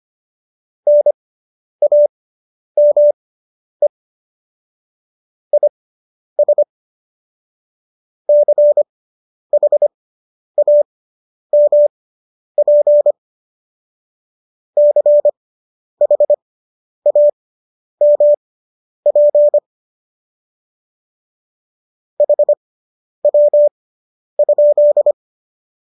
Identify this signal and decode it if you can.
Morse code, heard as a keyed tone, about 25 wpm, spaced as for 10 wpm, 590 Hz. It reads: NAME IS CHAMP CHAMP HW?